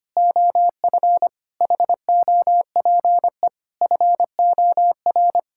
Morse code, heard as a keyed tone, about 25 wpm, 700 Hz. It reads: OF HOPE FOR
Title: Sentences from Top 1000 Words - Review Part 5 - 25wpm